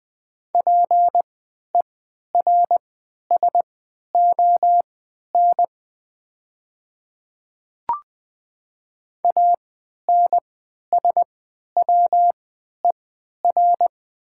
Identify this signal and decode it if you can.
Morse code, heard as a keyed tone, about 20 wpm, spaced as for 11 wpm, 700 Hz. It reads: PERSON ANSWER
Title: 3x Character Spacing - Top 500 Words - 20wpm